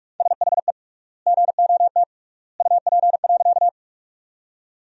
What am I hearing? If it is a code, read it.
Morse code, heard as a keyed tone, about 41 wpm, 700 Hz. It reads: SHE GOT UP.
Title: 3x Word-Spacing - Sentences from Top 300 Words - 45wpm